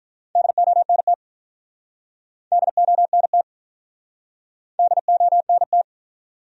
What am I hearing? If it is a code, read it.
Morse code, heard as a keyed tone, about 51 wpm, 700 Hz. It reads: DONT DONT DONT